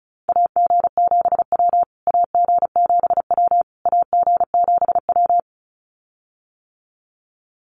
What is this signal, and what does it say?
Morse code, heard as a keyed tone, about 34 wpm, 700 Hz. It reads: AG7W AG7W AG7W
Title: USA Call Signs Repeated 3x 35wpm